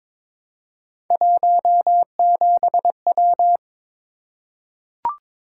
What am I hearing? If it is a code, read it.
Morse code, heard as a keyed tone, about 22 wpm, 700 Hz. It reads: 17W
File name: practice / 22wpm / Three Letter-Number Combinations - 22wpm